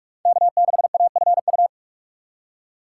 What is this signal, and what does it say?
Morse code, heard as a keyed tone, about 44 wpm, 690 Hz. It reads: K6AFU
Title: USA Call Signs - 5 Characters - 2x Character Spacing - 45wpm